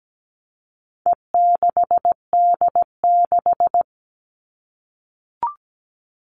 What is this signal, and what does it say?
Morse code, heard as a keyed tone, about 17 wpm, 700 Hz. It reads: E6D6
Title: All Cut Numbers - 4-Digit 17wpm